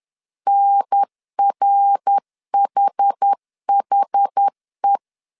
Morse code, heard as a keyed tone, about 10 wpm, 800 Hz. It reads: NRHHE
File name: CW / DET / 1 3 1500C S C 10wpm 3335